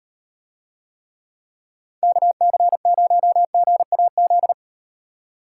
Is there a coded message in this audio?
Yes: Morse code from 2.0 to 4.5 s, heard as a keyed tone, about 38 wpm, 700 Hz. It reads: KC0GAZ